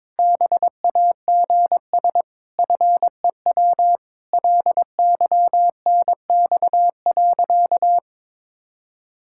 Morse code, heard as a keyed tone, about 22 wpm, 700 Hz. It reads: BAGS FEW LYNX.
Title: Pangram Challenge v2 - 22wpm